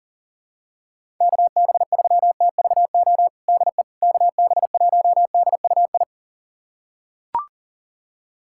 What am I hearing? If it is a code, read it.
Morse code, heard as a keyed tone, about 40 wpm, 700 Hz. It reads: KB3TVO DE KB1DUI
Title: USA Callsign DE Callsign - 6 Characters - 40wpm